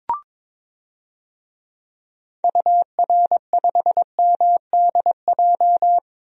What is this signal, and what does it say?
Morse code, heard as a keyed tone, about 21 wpm, 700 Hz. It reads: UR5MDJ
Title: DXCC-100 Call Signs - 6 Characters - Repeated 3x - 22wpm